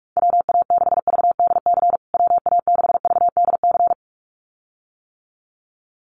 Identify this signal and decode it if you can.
Morse code, heard as a keyed tone, about 44 wpm, 690 Hz. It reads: WA6VDC WA6VDC
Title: USA Call Signs Repeated 2x 45wpm